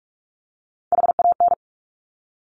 Morse code, heard as a keyed tone, about 46 wpm, 700 Hz. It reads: HAN
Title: Top Three Letter Combinations in Words 45wpm